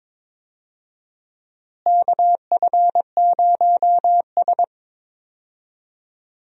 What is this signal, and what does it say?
Morse code, heard as a keyed tone, about 22 wpm, 700 Hz. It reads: KF0S
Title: USA Call Signs - 4 Characters - Repeated 3x - 22wpm